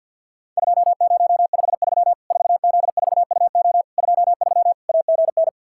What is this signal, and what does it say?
Morse code, heard as a keyed tone, about 50 wpm, 700 Hz. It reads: 2053 4Z4UO 23 AGN
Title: Worked All Europe DX Contest - QTC Exchange with Repeats - 50wpm